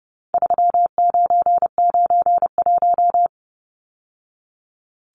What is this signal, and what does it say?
Morse code, heard as a keyed tone, about 29 wpm, 700 Hz. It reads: 3991
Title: Numbers - 4-Digit 30wpm